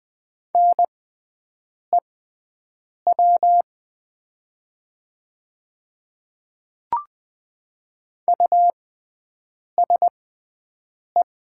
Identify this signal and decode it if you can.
Morse code, heard as a keyed tone, about 20 wpm, spaced as for 7 wpm, 700 Hz. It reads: NEW USE
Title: 6x Character Spacing - Top 300 Words - 20wpm